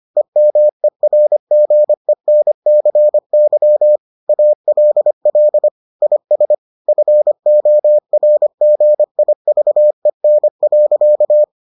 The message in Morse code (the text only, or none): EMERGENCY ALL IS FORGIVEN.